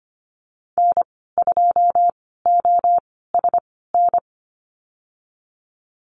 Morse code, heard as a keyed tone, about 25 wpm, spaced as for 16 wpm, 700 Hz. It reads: N2OSN
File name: Lesson 15g - Number 5 - Call Signs 25wpm